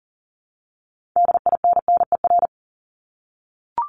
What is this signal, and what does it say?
Morse code, heard as a keyed tone, about 40 wpm, 700 Hz. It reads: DINNER
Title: Top 1751-2000 Words 40wpm